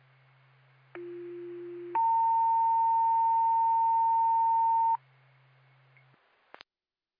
Two-tone page, heard as a one-second tone, then a three-second tone, about 350 then 910 Hz. It is heard from 1.0 to 5.0 s.